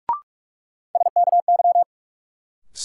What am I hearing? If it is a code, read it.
Morse code, heard as a keyed tone, about 45 wpm, 700 Hz. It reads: SKY
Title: Sentences from Top 1200 Words - New Word Drill - 45wpm